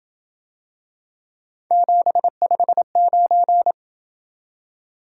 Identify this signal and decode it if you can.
Morse code, heard as a keyed tone, about 28 wpm, 700 Hz. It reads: 759